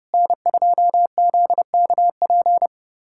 Morse code, heard as a keyed tone, about 30 wpm, 700 Hz. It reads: N2ZKP
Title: USA Callsign DE Callsign - 5 Characters - 30wpm